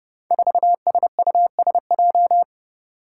Morse code, heard as a keyed tone, about 30 wpm, 700 Hz. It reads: 4SUSJ
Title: Five Challenging Letter-Number Combinations 30wpm